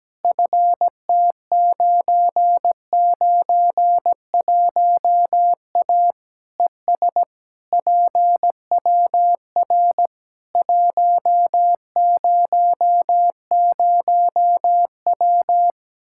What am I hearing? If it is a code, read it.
Morse code, heard as a keyed tone, about 17 wpm, 700 Hz. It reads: FT991A ES PWR 100W
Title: K1USN SST QSO Practice 17wpm